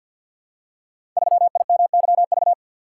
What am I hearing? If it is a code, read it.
Morse code, heard as a keyed tone, about 52 wpm, 700 Hz. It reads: 3IMYV
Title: Code Groups - 5 Letter:Numbers - 50wpm